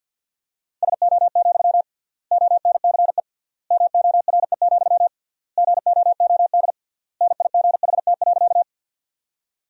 Morse code, heard as a keyed tone, about 50 wpm, 700 Hz. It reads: SO, ONCE MORE, GOOD NIGHT.
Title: 3x Word-Spacing - Sentences from Top 300 Words - 50wpm